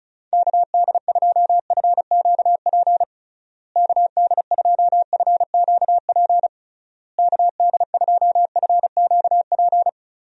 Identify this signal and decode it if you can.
Morse code, heard as a keyed tone, about 36 wpm, 700 Hz. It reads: KD2FQP KD2FQP KD2FQP
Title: USA Call Signs - 6 Characters - Repeated 3x - 35wpm